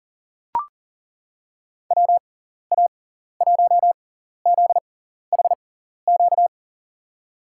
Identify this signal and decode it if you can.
Morse code, heard as a keyed tone, about 41 wpm, spaced as for 14 wpm, 700 Hz. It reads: WA1ZHQ
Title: USA Call Signs - 6 Characters - 6x Character Spacing - 40wpm